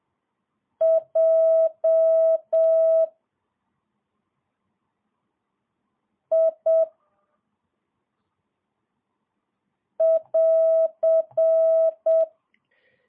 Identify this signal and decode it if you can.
Morse code, heard as a keyed tone, about 7 wpm, 650 Hz. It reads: J I <AR>
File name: jh8cbh / 2022cw